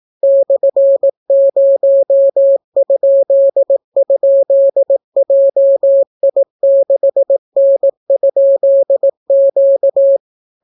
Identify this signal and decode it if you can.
Morse code, heard as a keyed tone, about 18 wpm, 550 Hz. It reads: /0??JI6N?Q